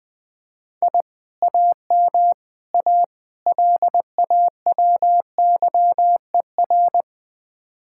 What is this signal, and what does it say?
Morse code, heard as a keyed tone, about 21 wpm, 700 Hz. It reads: I AM A LAWYER
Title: Speed-Racing Common QSO Phrases Encore - 20wpm